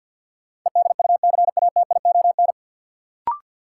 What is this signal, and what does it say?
Morse code, heard as a keyed tone, about 46 wpm, 700 Hz. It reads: EDUCATION
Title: Sentences from Top 1100 Words - New Word Drill Alt - 50wpm